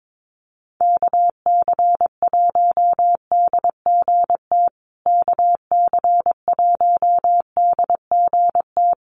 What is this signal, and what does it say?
Morse code, heard as a keyed tone, about 22 wpm, 690 Hz. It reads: KC1DGT KC1DGT
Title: USA Call Signs Repeated 3x 22wpm